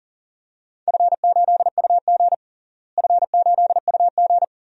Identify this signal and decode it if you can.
Morse code, heard as a keyed tone, about 40 wpm, 700 Hz. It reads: F8UG F8UG
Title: DXCC-100 Call Signs - 4 Characters - Repeated 2x - 40wpm